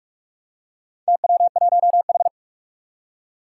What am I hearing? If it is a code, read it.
Morse code, heard as a keyed tone, about 46 wpm, 700 Hz. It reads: TW1H
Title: Four Challenging Letter-Number Combinations 45wpm